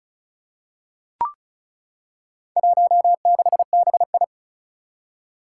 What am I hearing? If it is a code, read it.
Morse code, heard as a keyed tone, about 34 wpm, 700 Hz. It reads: 16BI
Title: Four Challenging Letter-Number Combinations 35wpm